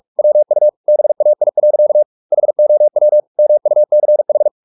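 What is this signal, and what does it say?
Morse code, heard as a keyed tone, about 44 wpm, 600 Hz. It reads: WU BAI. HOW MUCH
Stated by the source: Morse practice at 45 wpm, the novel Never by Ken Follett in English, from CW med Gnister.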